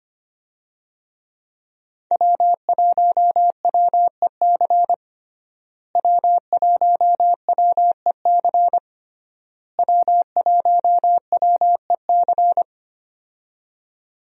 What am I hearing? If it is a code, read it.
Morse code, heard as a keyed tone, about 25 wpm, 700 Hz. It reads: W1WEC W1WEC W1WEC
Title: USA Call Signs - 5 Characters - Repeated 3x - 25wpm